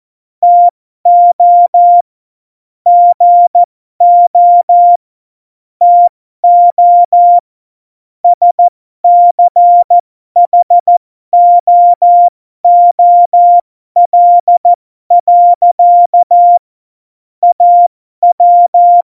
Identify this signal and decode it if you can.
Morse code, heard as a keyed tone, about 14 wpm, spaced as for 12 wpm, 700 Hz. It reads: TO GO TO SCHOOL. AW
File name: Morsekurs / Witze